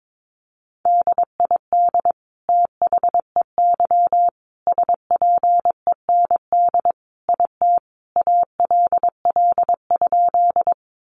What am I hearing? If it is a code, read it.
Morse code, heard as a keyed tone, about 22 wpm, 700 Hz. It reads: DID THEY SPEND IT ALL?
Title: Sentences from Top 800 Words 22wpm